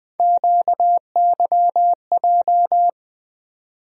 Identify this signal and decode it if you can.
Morse code, heard as a keyed tone, about 20 wpm, 700 Hz. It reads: QYJ
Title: USA Call Sign Postfixes - 3 Letters 20wpm